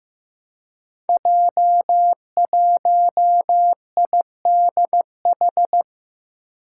Morse code, heard as a keyed tone, about 15 wpm, 700 Hz. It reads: J1IDH
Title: Five Challenging Letter-Number Combinations 15wpm